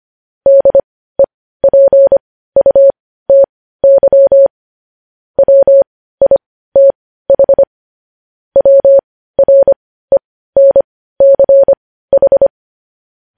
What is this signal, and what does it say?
Morse code, heard as a keyed tone, about 25 wpm, spaced as for 15 wpm, 550 Hz. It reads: DEPUTY WITH WRENCH
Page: news-headlines-in-morse-code-at-15-wpm